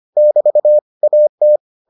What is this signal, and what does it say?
Morse code, heard as a keyed tone, about 25 wpm, 600 Hz. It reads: <BT> AT